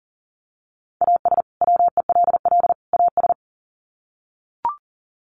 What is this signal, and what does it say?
Morse code, heard as a keyed tone, about 40 wpm, 700 Hz. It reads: AS WELL AS
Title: Sets of 3 Words 40wpm